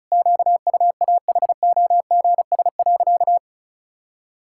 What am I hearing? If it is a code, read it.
Morse code, heard as a keyed tone, about 36 wpm, 700 Hz. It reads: QUAHOGS.
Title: Pangram Challenge v2 - 35wpm